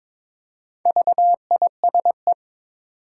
Morse code, heard as a keyed tone, about 23 wpm, 690 Hz. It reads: VISE
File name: Difficult Words with Hints - 22wpm